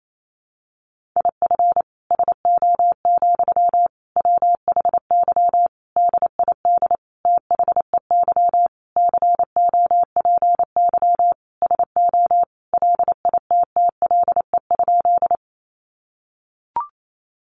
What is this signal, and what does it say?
Morse code, heard as a keyed tone, about 28 wpm, 700 Hz. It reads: IF SO, WHY DID THEY COPY SO LITTLE?
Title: Sentences from Top 1400 Words 28wpm